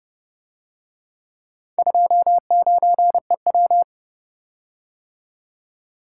Morse code, heard as a keyed tone, about 29 wpm, 700 Hz. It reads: 29EW